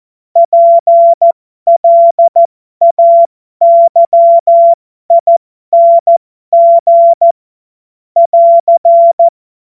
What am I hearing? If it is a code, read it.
Morse code, heard as a keyed tone, about 14 wpm, spaced as for 12 wpm, 680 Hz. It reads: PLAYING <AR>
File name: Morsekurs / Witze